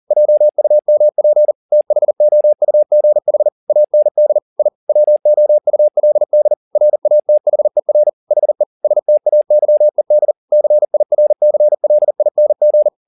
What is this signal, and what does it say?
Morse code, heard as a keyed tone, about 40 wpm, 600 Hz. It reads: JUMP THOUGH AND I WOULD RATHER HE STAYED CIRCLING
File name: the old man and the sea 2 40wpm